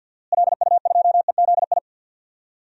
Morse code, heard as a keyed tone, about 51 wpm, 700 Hz. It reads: LU2EZI
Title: DXCC-100 Call Signs - 6 Characters - Repeated 2x - 50wpm